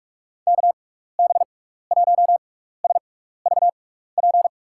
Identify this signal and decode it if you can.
Morse code, heard as a keyed tone, about 46 wpm, spaced as for 15 wpm, 700 Hz. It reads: KB1SVP
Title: USA Call Signs - 6x Character Spacing - 45wpm